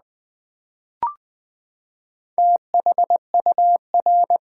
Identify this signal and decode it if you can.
Morse code, heard as a keyed tone, about 20 wpm, 700 Hz. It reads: THUR
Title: Build A Word - 4 Letters At A Time - 8-Letter Words - 20wpm